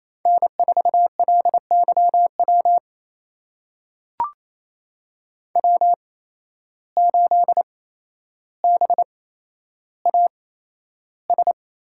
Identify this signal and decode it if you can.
Morse code, heard as a keyed tone, about 28 wpm, spaced as for 12 wpm, 700 Hz. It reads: N4LYW W8BAS
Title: USA Call Signs - 5 Characters - 8x Character Spacing - 28wpm